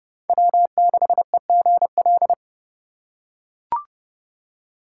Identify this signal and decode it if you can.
Morse code, heard as a keyed tone, about 29 wpm, 700 Hz. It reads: W6EGL